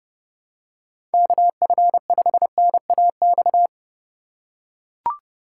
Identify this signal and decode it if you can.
Morse code, heard as a keyed tone, about 30 wpm, 700 Hz. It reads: KF5NAX
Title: USA Call Signs - 6 Characters - 2x Character Spacing - 30wpm